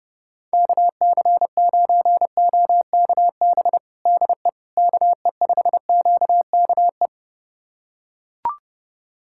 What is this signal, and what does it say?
Morse code, heard as a keyed tone, about 30 wpm, 700 Hz. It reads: KC9OKB DE KE5QKE
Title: USA Callsign DE Callsign - 6 Characters - 30wpm